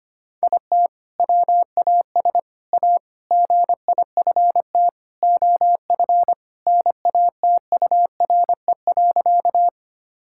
Morse code, heard as a keyed tone, about 25 wpm, 700 Hz. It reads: IT WAS A GIFT OF NATURE.
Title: Sentences from Top 1000 Words - Review Part 6 - 25wpm